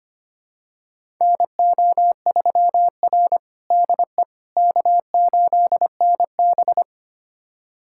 Morse code, heard as a keyed tone, about 25 wpm, 700 Hz. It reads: NO3R DE K8NB
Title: USA Callsign DE Callsign - 4 Characters - 25wpm